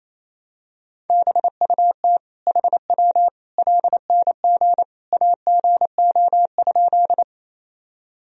Morse code, heard as a keyed tone, about 28 wpm, 700 Hz. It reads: BUT HW LNG AGO?
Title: Sentences from Top 500 Words and Abbreviations 28wpm